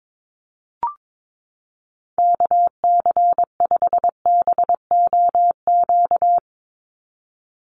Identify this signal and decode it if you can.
Morse code, heard as a keyed tone, about 22 wpm, 700 Hz. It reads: KC5BOQ
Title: USA Call Signs 22wpm